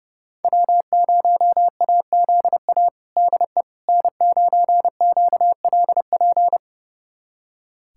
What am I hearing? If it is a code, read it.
Morse code, heard as a keyed tone, about 32 wpm, 700 Hz. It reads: W0AZA DE N9QLP
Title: USA Callsign DE Callsign - 5 Characters - 30wpm